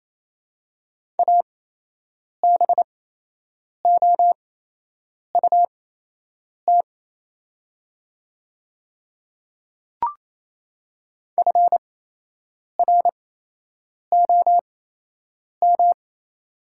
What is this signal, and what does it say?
Morse code, heard as a keyed tone, about 28 wpm, spaced as for 8 wpm, 700 Hz. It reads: ABOUT FROM
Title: 8x Character Spacing - Top 300 Words - 28wpm